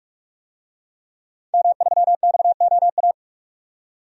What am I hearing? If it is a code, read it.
Morse code, heard as a keyed tone, about 44 wpm, 700 Hz. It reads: M3XOA
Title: DXCC-100 Call Signs - 5 Characters - 45wpm